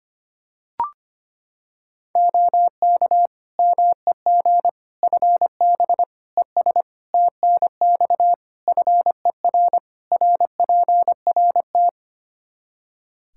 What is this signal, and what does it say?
Morse code, heard as a keyed tone, about 25 wpm, 700 Hz. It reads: OK MEG FB ES TNX FER RPRT
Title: Practice QSO - Exchange 2 - Info - 25wpm